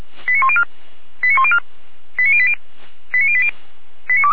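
Five-tone selective call (ZVEI), heard as quick runs of five tones, tones of about 70 ms. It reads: 89104 89104 89080 89080
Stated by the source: ZVEI five-tone selective call 89333 received by SDR.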